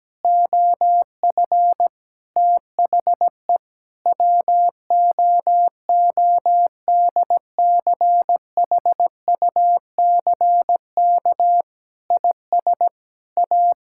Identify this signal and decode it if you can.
Morse code, heard as a keyed tone, about 17 wpm, 700 Hz. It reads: OF THE WOODCHUCK IS A
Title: Pangram Challenge v2 - 17wpm